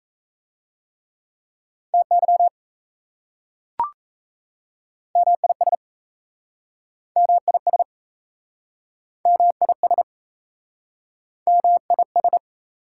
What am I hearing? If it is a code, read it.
Morse code, heard as a keyed tone, about 38 wpm, 700 Hz. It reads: TY MIS MIS MIS MIS